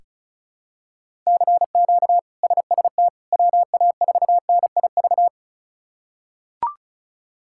Morse code, heard as a keyed tone, about 36 wpm, 700 Hz. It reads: CQ SST WA4NIV
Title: K1USN SST QSO Practice - 35wpm